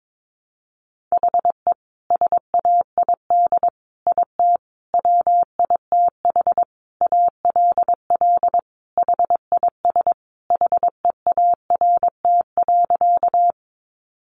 Morse code, heard as a keyed tone, about 22 wpm, 700 Hz. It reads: HE SAID IT WITH ALL HIS HEART.
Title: Sentences from Top 1200 Words 22wpm